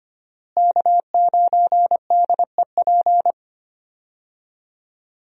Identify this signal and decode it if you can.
Morse code, heard as a keyed tone, about 25 wpm, 700 Hz. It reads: K9DEP
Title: USA Call Signs - 5 Characters - Repeated 3x - 25wpm